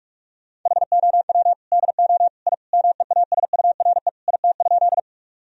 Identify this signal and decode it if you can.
Morse code, heard as a keyed tone, about 46 wpm, 700 Hz. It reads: HOW DO I MEASURE IT?
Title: Sentences from Top 1000 Words - Review Part 2 - 45wpm